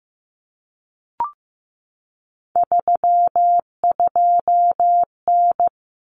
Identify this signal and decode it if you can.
Morse code, heard as a keyed tone, about 15 wpm, 700 Hz. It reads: 32N
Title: Common Cut Numbers - 3-Digit 15wpm